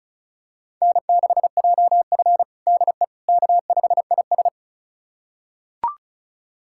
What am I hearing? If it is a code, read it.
Morse code, heard as a keyed tone, about 36 wpm, 690 Hz. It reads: N6JF DE K5IS
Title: USA Callsign DE Callsign - 4 Characters - 35wpm